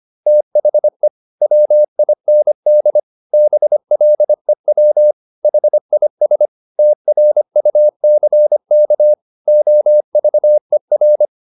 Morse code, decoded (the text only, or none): THE WIND BLEW HIS TRUCK OVER